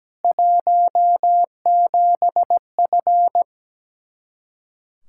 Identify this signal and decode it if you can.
Morse code, heard as a keyed tone, about 17 wpm, 700 Hz. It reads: 17F
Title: Practice QSO - Exchange 2 - Info - 17wpm